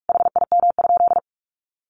Morse code, heard as a keyed tone, about 44 wpm, 700 Hz. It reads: HIM?
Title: Sentences from Top 200 Words 45wpm